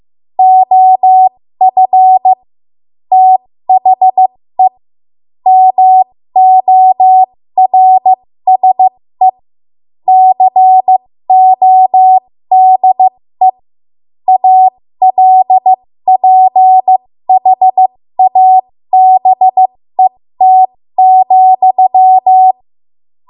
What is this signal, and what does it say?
Morse code, heard as a keyed tone, about 15 wpm, spaced as for 13 wpm, 750 Hz. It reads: OF THE MORSE CODE ALPHABET,
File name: Morse / 13WPM